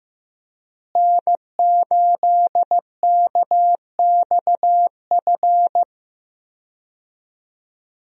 Morse code, heard as a keyed tone, about 15 wpm, 700 Hz. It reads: N8KXF
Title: USA Call Signs - 5 Characters - Repeated 3x - 15wpm